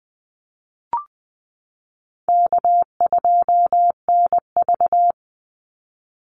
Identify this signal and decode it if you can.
Morse code, heard as a keyed tone, about 20 wpm, 700 Hz. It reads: K2NV
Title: Call Signs Repeated 3x - 20wpm